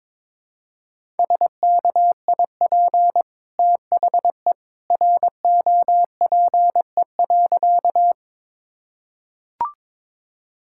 Morse code, heard as a keyed tone, about 22 wpm, 690 Hz. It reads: SKIP THE ROPE.
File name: practice / 22wpm / Easy Sentences with Hints - 22wpm